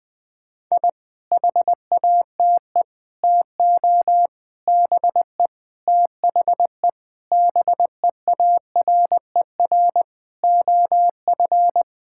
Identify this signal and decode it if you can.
Morse code, heard as a keyed tone, about 21 wpm, 700 Hz. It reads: I HATE TO BE THE BEARER OF
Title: Sets of 9 Words - 20wpm